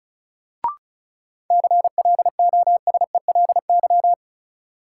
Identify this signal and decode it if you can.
Morse code, heard as a keyed tone, about 34 wpm, 700 Hz. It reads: CLOSELY
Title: Sentences from Top 1400 Words - New Word Drill - 35wpm